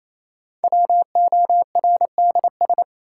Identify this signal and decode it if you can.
Morse code, heard as a keyed tone, about 27 wpm, 700 Hz. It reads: WORDS
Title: Sentences from Top 1000 Words - Review Part 6 - New Word Drill - 28wpm